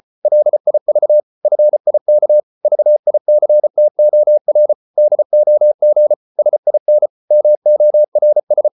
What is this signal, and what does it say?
Morse code, heard as a keyed tone, about 34 wpm, 600 Hz. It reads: LIV FIK VICTOR DOG SIN MORS
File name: Ferie QRQ 34wpm